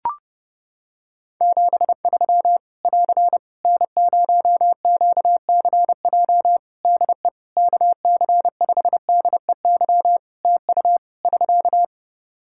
Morse code, heard as a keyed tone, about 30 wpm, 700 Hz. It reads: 73 <AR> N0QCJ DE KC5DEY TU <SK>
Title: Practice QSO - Exchange 4 - Ending - 30wpm